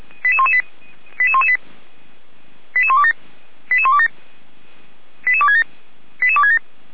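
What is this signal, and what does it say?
Five-tone selective call (ZVEI), heard as quick runs of five tones, tones of about 80 ms. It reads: 88118 88127 88127 88267 88267
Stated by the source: ZVEI five-tone call 88129 received by SDR, recorded by Feuerwehr Orlamünde.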